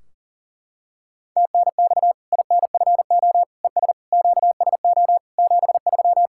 Morse code, heard as a keyed tone, about 40 wpm, 700 Hz. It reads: TNX INFO ES QSO 73
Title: Practice QSO - Exchange 4 - Ending - 40wpm